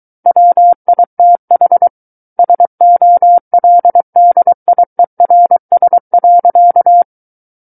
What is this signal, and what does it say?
Morse code, heard as a keyed tone, about 23 wpm, 700 Hz. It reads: WITH SOLDIERS.
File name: Book One / 14-Chapter 14 - 23 WPM